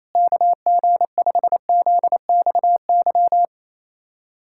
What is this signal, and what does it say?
Morse code, heard as a keyed tone, about 28 wpm, 700 Hz. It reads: KG5ZXY
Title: USA Callsign DE Callsign 28wpm